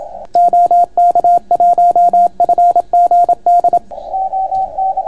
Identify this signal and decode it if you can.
Morse code, heard as a keyed tone, about 27 wpm, 690 Hz, with radio noise. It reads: OK1FGD